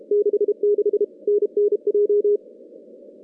Morse code, heard as a keyed tone, about 32 wpm, 410 Hz. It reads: 66 NNJ
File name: contests / CW-SS-k5tr- -2100